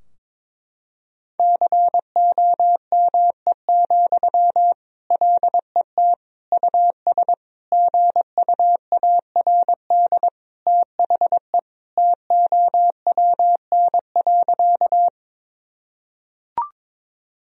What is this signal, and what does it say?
Morse code, heard as a keyed tone, about 21 wpm, 700 Hz. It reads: COME, LET US GUARD THE TOWN.
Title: Sentences from Top 1000 Words - Review Part 4 - 22wpm